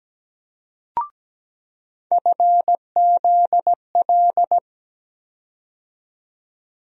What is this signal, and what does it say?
Morse code, heard as a keyed tone, about 17 wpm, 700 Hz. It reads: FZL
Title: USA Call Sign Postfixes - 3 Letters 17wpm